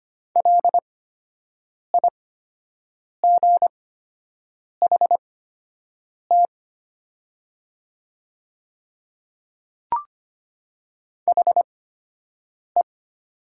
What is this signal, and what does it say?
Morse code, heard as a keyed tone, about 25 wpm, spaced as for 7 wpm, 700 Hz. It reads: LIGHT HE
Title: 8x Character Spacing - Top 200 Words - 25wpm